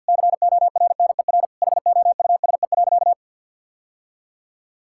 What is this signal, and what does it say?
Morse code, heard as a keyed tone, about 50 wpm, 700 Hz. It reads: CORNER HOUSE.
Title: Sentences from Top 1600 Words 50wpm